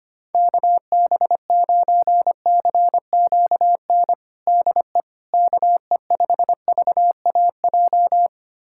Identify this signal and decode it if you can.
Morse code, heard as a keyed tone, about 25 wpm, 700 Hz. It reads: KB9CQN DE KE5VAJ
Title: USA Callsign DE Callsign - 6 Characters - 25wpm